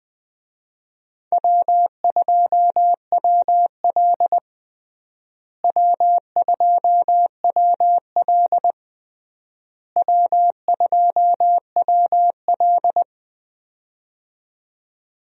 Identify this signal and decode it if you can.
Morse code, heard as a keyed tone, about 20 wpm, 700 Hz. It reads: W2WL W2WL W2WL
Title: USA Call Signs - 4 Characters - Repeated 3x - 20wpm